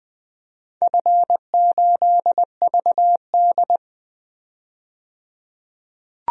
Morse code, heard as a keyed tone, about 20 wpm, 700 Hz. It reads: F8VD
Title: DXCC-100 Call Signs - 4 Characters - 20wpm